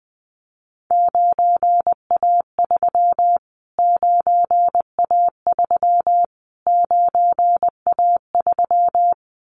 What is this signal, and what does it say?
Morse code, heard as a keyed tone, about 20 wpm, 700 Hz. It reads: 9A3 9A3 9A3
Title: Callsign Prefixes for Top 25 DXCC - Repeated 3x - 20wpm